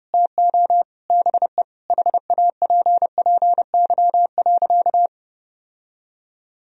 Morse code, heard as a keyed tone, about 30 wpm, 700 Hz. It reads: TO BE HAPPY.
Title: Sentences from Top 1000 Words - Review Part 1 - 30wpm